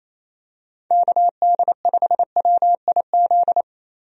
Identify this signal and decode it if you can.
Morse code, heard as a keyed tone, about 28 wpm, 700 Hz. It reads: KD5WIZ
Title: USA Call Signs - 6 Characters - Repeated 2x - 28wpm